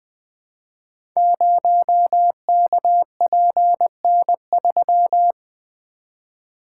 Morse code, heard as a keyed tone, about 20 wpm, 700 Hz. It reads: 0KPN3
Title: Code Groups - 5 Letter:Numbers - 20wpm